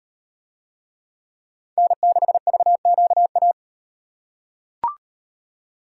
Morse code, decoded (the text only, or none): N6VQA